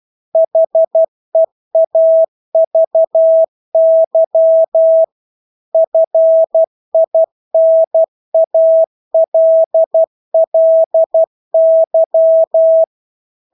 Morse code, heard as a keyed tone, about 12 wpm, 650 Hz. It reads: HEAVY FINALLY
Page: Engelske ord 12 wpm